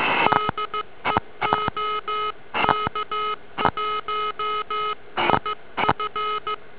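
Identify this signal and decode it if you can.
Morse code, heard as a keyed tone, about 15 wpm, 1270 Hz, with noise. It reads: DE OK1IF
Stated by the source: Amateur radio DX contact.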